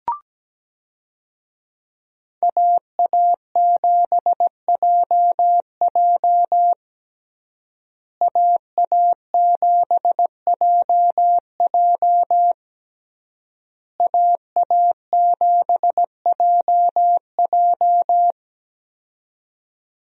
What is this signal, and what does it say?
Morse code, heard as a keyed tone, about 17 wpm, 700 Hz. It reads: AA7JJ AA7JJ AA7JJ
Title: USA Call Signs - 5 Characters - Repeated 3x - 17wpm